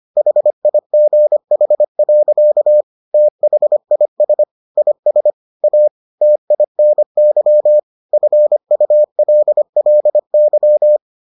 Morse code, heard as a keyed tone, about 25 wpm, 600 Hz. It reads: HIGH. THIS IS A TINY FULLY